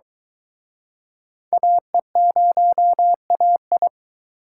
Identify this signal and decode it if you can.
Morse code, heard as a keyed tone, about 24 wpm, 700 Hz. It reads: AE0AI